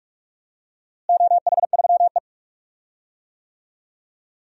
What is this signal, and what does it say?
Morse code, heard as a keyed tone, about 45 wpm, 700 Hz. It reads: OH3E
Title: DXCC-100 Call Signs - 4 Characters - Repeated 3x - 45wpm